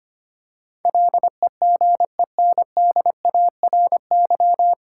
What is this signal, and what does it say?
Morse code, heard as a keyed tone, about 25 wpm, 700 Hz. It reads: LEGENDARY
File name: Nine Letter Words - 25wpm